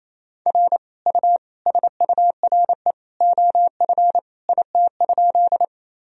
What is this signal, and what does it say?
Morse code, heard as a keyed tone, about 28 wpm, 700 Hz. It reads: R U SURE OF IT?
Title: Sentences from Top 300 Words and Abbreviations 28wpm